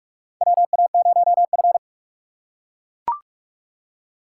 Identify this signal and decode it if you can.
Morse code, heard as a keyed tone, about 43 wpm, 700 Hz. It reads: WA0F
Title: USA Call Signs - 3x Character Spacing - 45wpm